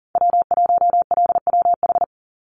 Morse code, heard as a keyed tone, about 40 wpm, 700 Hz. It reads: W1LWH
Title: Call Signs Repeated 2x - 40wpm